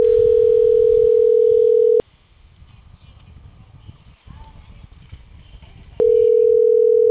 Ringback tone (North America), 440 + 480 Hz.
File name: CENTRAL CAEA / out agcaea02 - .35045